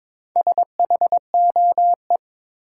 Morse code, heard as a keyed tone, about 22 wpm, 700 Hz. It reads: SHOE
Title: Four Letter Words - 22wpm